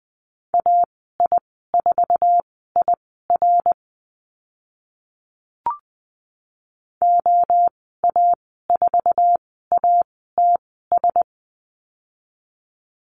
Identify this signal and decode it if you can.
Morse code, heard as a keyed tone, about 20 wpm, spaced as for 14 wpm, 700 Hz. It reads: AI4IR OA4ATS
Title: 055 Lesson 10g - Letter r - Call Signs 20wpm